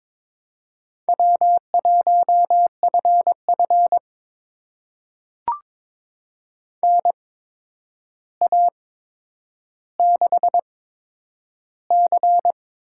Morse code, heard as a keyed tone, about 22 wpm, spaced as for 9 wpm, 700 Hz. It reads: W1FF NA6C
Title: USA Call Signs - 4 Characters - 8x Character Spacing - 22wpm